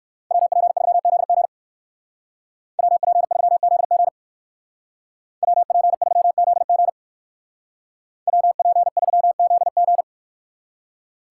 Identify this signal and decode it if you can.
Morse code, heard as a keyed tone, about 54 wpm, 700 Hz. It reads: WP3ZG WP3ZG WP3ZG WP3ZG
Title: Speed Racing - USA Call Signs 45wpm